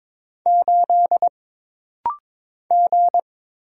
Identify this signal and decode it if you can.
Morse code, heard as a keyed tone, about 22 wpm, 700 Hz. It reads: 8 G
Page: This rapid-fire timing is geared to push your ICR (Instant Character Recognition) skills.